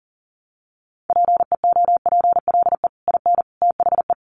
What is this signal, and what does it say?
Morse code, heard as a keyed tone, about 40 wpm, 700 Hz. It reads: PEOPLE IN THE
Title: Sets of 5 Words Encore 40wpm